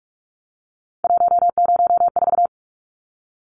Morse code, heard as a keyed tone, about 44 wpm, 700 Hz. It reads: 104
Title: Numbers - 3-Digit 45wpm